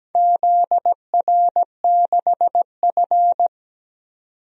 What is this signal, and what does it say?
Morse code, heard as a keyed tone, about 17 wpm, 700 Hz. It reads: ZR6F
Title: DXCC-100 Call Signs - 4 Characters - 17wpm